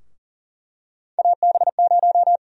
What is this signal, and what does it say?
Morse code, heard as a keyed tone, about 39 wpm, 700 Hz. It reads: AB0
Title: USA Call Sign Prefixes - Repeated 2x - 40wpm